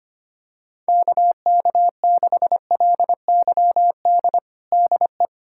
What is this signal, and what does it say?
Morse code, heard as a keyed tone, about 25 wpm, 700 Hz. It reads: KK6LYD DE
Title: USA Callsign DE Callsign - 6 Characters - 25wpm